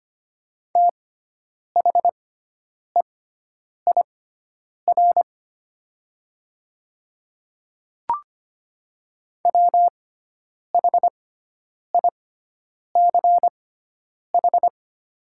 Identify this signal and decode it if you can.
Morse code, heard as a keyed tone, about 25 wpm, spaced as for 9 wpm, 700 Hz. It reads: THEIR WHICH
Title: 6x Character Spacing - Top 300 Words - 25wpm